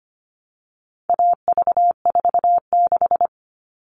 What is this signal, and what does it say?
Morse code, heard as a keyed tone, about 25 wpm, 700 Hz. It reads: AV46
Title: All Cut Numbers - 4-Digit 25wpm